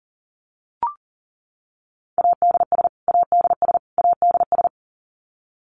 Morse code, heard as a keyed tone, about 40 wpm, 700 Hz. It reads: ADS ADS ADS
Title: Abbreviations Repeated 3x - 40wpm